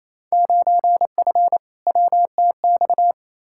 Morse code, heard as a keyed tone, about 28 wpm, 700 Hz. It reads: 9F WTX
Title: Field Day Exchanges Encore - 28wpm